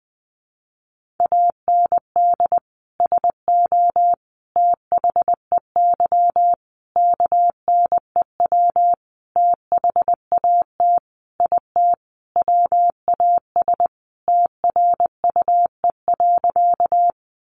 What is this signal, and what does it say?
Morse code, heard as a keyed tone, about 20 wpm, 700 Hz. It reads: AND SO THEY KNEW THAT IT WAS TRUE.
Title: Sentences from Top 400 Words 20wpm